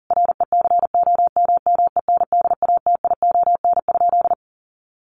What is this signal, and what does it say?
Morse code, heard as a keyed tone, about 40 wpm, 700 Hz. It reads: RECOMMENDATION?
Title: Sentences from Top 1400 Words 40wpm